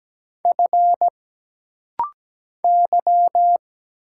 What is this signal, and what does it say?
Morse code, heard as a keyed tone, about 17 wpm, 700 Hz. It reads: F Y
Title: Single Letter-Number Alt - Rapid-Fire 17wpm